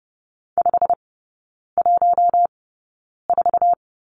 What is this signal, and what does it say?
Morse code, heard as a keyed tone, about 30 wpm, spaced as for 9 wpm, 700 Hz. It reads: 514
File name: course / 30:10wpm / 087 Lesson 15e - Number 5 - Three Characters 30wpm